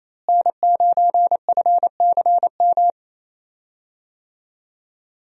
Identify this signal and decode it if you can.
Morse code, heard as a keyed tone, about 27 wpm, 700 Hz. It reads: N9FCM
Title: USA Call Signs - 5 Characters - Repeated 2x - 28wpm